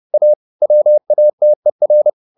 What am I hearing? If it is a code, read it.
Morse code, heard as a keyed tone, about 30 wpm, 600 Hz. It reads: A WATER